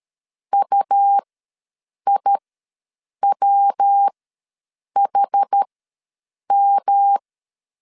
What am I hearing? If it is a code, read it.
Morse code, heard as a keyed tone, about 13 wpm, spaced as for 7 wpm, 800 Hz. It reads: UIWHM
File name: LEZIONI / 7 GF9Z / 7 GF9Z ws 12wpm 3321